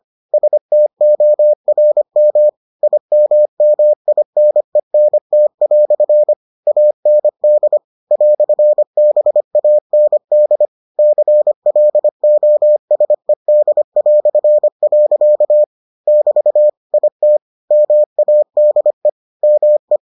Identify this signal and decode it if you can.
Morse code, heard as a keyed tone, about 25 wpm, 600 Hz. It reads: STORM IMMINENT" AND "BAND CLOSED". <BT> IT MADE ME